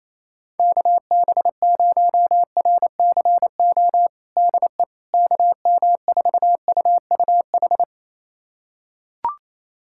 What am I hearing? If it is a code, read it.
Morse code, heard as a keyed tone, about 28 wpm, 700 Hz. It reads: KB0RCO DE KM4UUH
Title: USA Callsign DE Callsign - 6 Characters - 28wpm